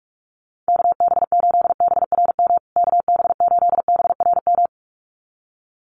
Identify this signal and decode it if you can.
Morse code, heard as a keyed tone, about 44 wpm, 700 Hz. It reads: KB8BRM KB8BRM
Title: USA Call Signs Repeated 2x 45wpm